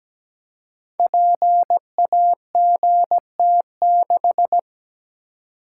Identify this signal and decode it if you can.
Morse code, heard as a keyed tone, about 17 wpm, 700 Hz. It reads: PAGT6
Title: Code Groups - 5 Letter:Numbers - 17wpm